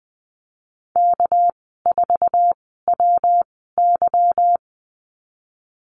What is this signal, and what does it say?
Morse code, heard as a keyed tone, about 20 wpm, spaced as for 15 wpm, 700 Hz. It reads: K4WY
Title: 200 Lesson 32g - Letter k - Call Signs 20wpm